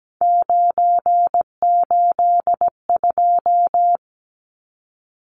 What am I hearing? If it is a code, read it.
Morse code, heard as a keyed tone, about 17 wpm, 700 Hz. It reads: 982
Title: Numbers - 3-Digit 17wpm